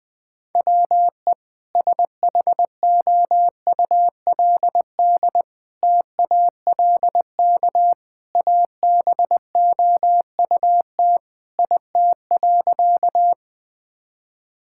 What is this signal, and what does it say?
Morse code, heard as a keyed tone, about 20 wpm, 700 Hz. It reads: WE SHOULD TALK ABOUT IT.
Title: Sentences with 2-Word Phrases - 20wpm